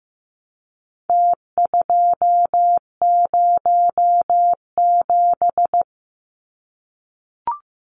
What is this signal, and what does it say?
Morse code, heard as a keyed tone, about 15 wpm, 700 Hz. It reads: T207
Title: Numbers - Contest Style - 4-Digit 15wpm